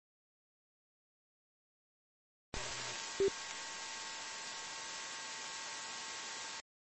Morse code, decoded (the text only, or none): none